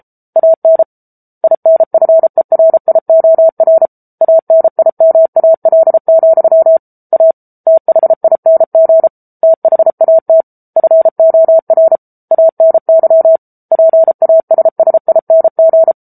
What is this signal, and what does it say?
Morse code, heard as a keyed tone, about 34 wpm, 650 Hz. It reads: AN INFERIOR ANIMAL, A THING THAT FOR ANY PASSING
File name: Book Two / 07-Chapter 7 - 33 WPM